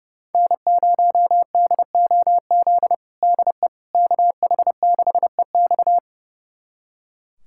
Morse code, heard as a keyed tone, about 30 wpm, 700 Hz. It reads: N0DOZ DE KH6EX
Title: USA Callsign DE Callsign - 5 Characters - 30wpm